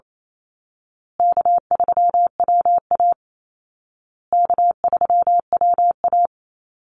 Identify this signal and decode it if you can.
Morse code, heard as a keyed tone, about 28 wpm, 700 Hz. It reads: K3WA K3WA
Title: Call Signs Repeated 2x - 28wpm